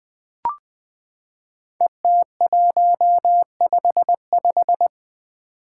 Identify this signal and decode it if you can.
Morse code, heard as a keyed tone, about 21 wpm, 700 Hz. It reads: ET155
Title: Five Challenging Letter-Number Combinations 20wpm